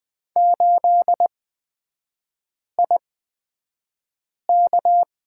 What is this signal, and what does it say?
Morse code, heard as a keyed tone, about 20 wpm, spaced as for 5 wpm, 690 Hz. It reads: 8IK
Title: 255 Lesson 41e - Period - Three Characters 20wpm